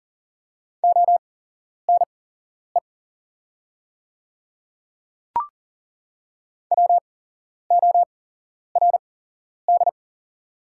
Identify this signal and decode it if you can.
Morse code, heard as a keyed tone, about 40 wpm, spaced as for 11 wpm, 700 Hz. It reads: ONE WORD
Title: 8x Character Spacing - Top 100 Words - 40wpm